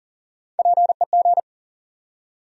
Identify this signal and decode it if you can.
Morse code, heard as a keyed tone, about 40 wpm, 700 Hz. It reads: PEG
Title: USA Common Names - Max 4 Letters - 40wpm